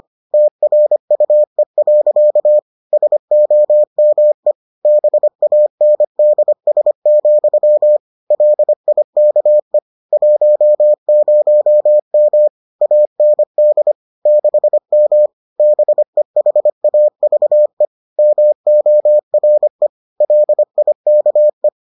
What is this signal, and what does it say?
Morse code, heard as a keyed tone, about 25 wpm, 600 Hz. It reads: TRUE. SOME BANDS, LIKE 10M AND 6M BEHAVE MORE LIKE